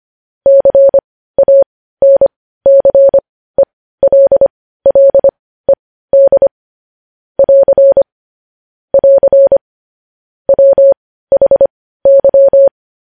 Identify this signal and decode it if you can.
Morse code, heard as a keyed tone, about 25 wpm, spaced as for 15 wpm, 550 Hz. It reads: CANCELLED <AR> <AR> WHY
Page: news headlines in morse code at 15 wpm